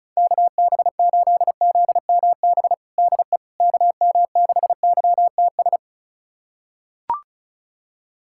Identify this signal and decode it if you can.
Morse code, heard as a keyed tone, about 36 wpm, 700 Hz. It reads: KB8ZMB DE KM6YTS
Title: USA Callsign DE Callsign - 6 Characters - 35wpm